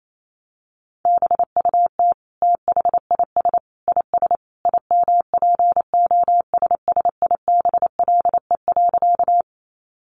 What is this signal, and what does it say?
Morse code, heard as a keyed tone, about 28 wpm, 700 Hz. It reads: BUT THIS IS IMPOSSIBLE.
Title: Sentences from Top 1100 Words 28wpm